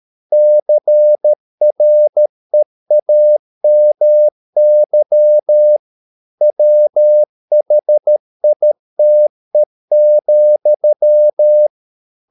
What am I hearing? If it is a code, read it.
Morse code, heard as a keyed tone, about 13 wpm, 600 Hz. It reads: CREAMY WHITE,